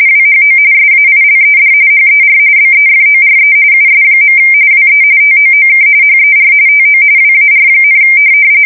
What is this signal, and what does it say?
RTTY, about 45.45 baud. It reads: WHO SAILED HER BEFORE US. WE TOOK HER TO WAR,
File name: fsk170 / USS-Missouri